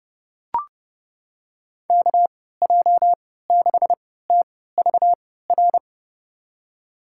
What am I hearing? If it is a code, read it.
Morse code, heard as a keyed tone, about 32 wpm, spaced as for 17 wpm, 700 Hz. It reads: KJ6TVR
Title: USA Call Signs - 6 Characters - 3x Character Spacing - 30wpm